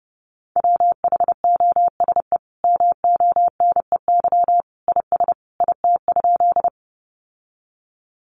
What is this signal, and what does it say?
Morse code, heard as a keyed tone, about 30 wpm, 700 Hz. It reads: WHOSE MONEY IS IT?
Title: Sentences from Top 1200 Words 30wpm